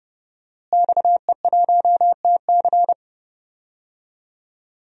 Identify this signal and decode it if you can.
Morse code, heard as a keyed tone, about 29 wpm, 700 Hz. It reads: XE1TC
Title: DXCC-100 Call Signs - 5 Characters - Repeated 3x - 30wpm